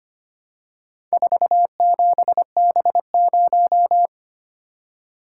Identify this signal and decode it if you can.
Morse code, heard as a keyed tone, about 25 wpm, 700 Hz. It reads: 47B0
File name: practice / 25wpm / Four Letter-Number Combinations - 25wpm